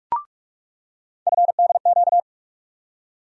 Morse code, heard as a keyed tone, about 41 wpm, 700 Hz. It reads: FDQ
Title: USA Call Sign Postfixes - 3 Letters 45wpm